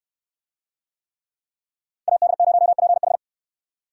Morse code, heard as a keyed tone, about 69 wpm, 700 Hz. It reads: AD0PH